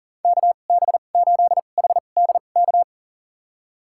Morse code, heard as a keyed tone, about 40 wpm, spaced as for 29 wpm, 700 Hz. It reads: KB8HDK
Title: USA Call Signs - 6 Characters - 2x Character Spacing - 40wpm